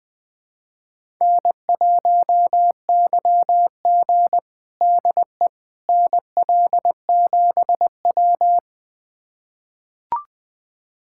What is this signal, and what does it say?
Morse code, heard as a keyed tone, about 20 wpm, 700 Hz. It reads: N1YG DE NL7W
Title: USA Callsign DE Callsign - 4 Characters - 20wpm